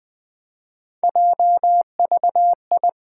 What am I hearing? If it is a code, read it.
Morse code, heard as a keyed tone, about 20 wpm, 700 Hz. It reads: JVI
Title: Three Letter-Number Combinations - 20wpm